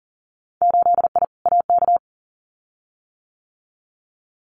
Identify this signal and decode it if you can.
Morse code, heard as a keyed tone, about 42 wpm, 700 Hz. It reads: 8I AK
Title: Winter Field Day Exchanges Encore 40wpm